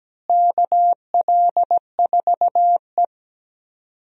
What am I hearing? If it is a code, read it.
Morse code, heard as a keyed tone, about 17 wpm, 700 Hz. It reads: KL4E
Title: USA Call Signs - 4 Characters - Repeated 2x - 17wpm